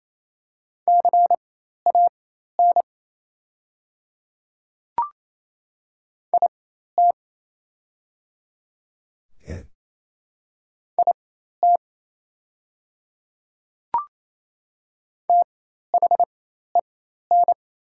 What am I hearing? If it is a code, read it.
Morse code, heard as a keyed tone, about 28 wpm, spaced as for 13 wpm, 700 Hz. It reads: CAN IT IT THEN
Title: 4x Character Spacing - Top 100 Words - 28wpm